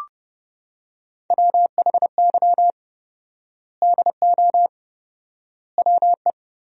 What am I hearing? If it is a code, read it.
Morse code, heard as a keyed tone, about 30 wpm, 700 Hz. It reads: WHY DO WE
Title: 4x Word-Spacing - Sentences from Top 400 Words - 30wpm